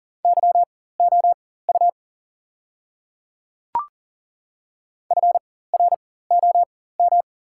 Morse code, heard as a keyed tone, about 37 wpm, spaced as for 18 wpm, 700 Hz. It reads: YOU FROM
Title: 4x Character Spacing - Top 100 Words - 40wpm